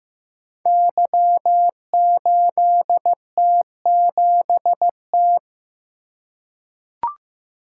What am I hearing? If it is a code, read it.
Morse code, heard as a keyed tone, about 15 wpm, 700 Hz. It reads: Y8T7T
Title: Code Groups - 5 Letter:Numbers - 15wpm